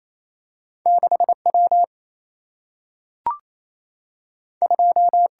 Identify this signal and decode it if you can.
Morse code, heard as a keyed tone, about 28 wpm, 700 Hz. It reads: 6W 2
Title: Single-Letter Single-Number 28wpm